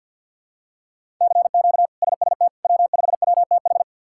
Morse code, heard as a keyed tone, about 52 wpm, 700 Hz. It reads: CQ SST W5WTH
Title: K1USN SST QSO Practice - 50wpm